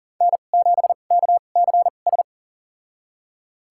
Morse code, heard as a keyed tone, about 39 wpm, spaced as for 28 wpm, 700 Hz. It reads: N7KCS
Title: USA Call Signs - 5 Characters - 2x Character Spacing - 40wpm